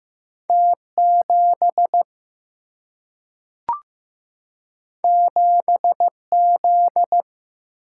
Morse code, heard as a keyed tone, about 15 wpm, 700 Hz. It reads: T 7 7 Z
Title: Single-Letter Single-Number 15wpm